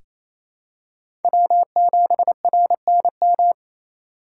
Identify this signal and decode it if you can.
Morse code, heard as a keyed tone, about 28 wpm, 700 Hz. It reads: W7RNM
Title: USA Call Signs - 5 Characters - 2x Character Spacing - 28wpm